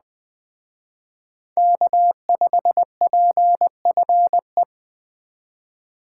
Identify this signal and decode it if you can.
Morse code, heard as a keyed tone, about 20 wpm, 700 Hz. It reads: K5PFE
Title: Speed Racing - USA 1x3 Call Signs 17wpm